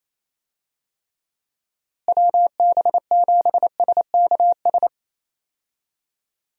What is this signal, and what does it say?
Morse code, heard as a keyed tone, about 28 wpm, 700 Hz. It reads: WB7SKS
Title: USA Call Signs - 6 Characters - Repeated 3x - 28wpm